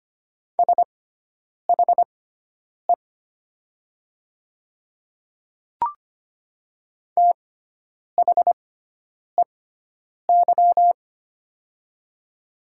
Morse code, heard as a keyed tone, about 25 wpm, spaced as for 9 wpm, 700 Hz. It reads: SHE THEY
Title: 6x Character Spacing - Top 200 Words - 25wpm